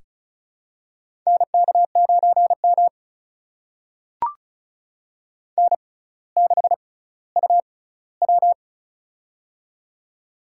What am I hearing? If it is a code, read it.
Morse code, heard as a keyed tone, about 36 wpm, spaced as for 18 wpm, 690 Hz. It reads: NK9M N6UW
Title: USA Call Signs - 4 Characters - 6x Character Spacing - 35wpm